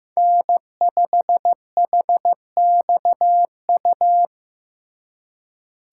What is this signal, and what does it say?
Morse code, heard as a keyed tone, about 18 wpm, 700 Hz. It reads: N5HXU
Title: Speed Racing - USA Call Signs 15wpm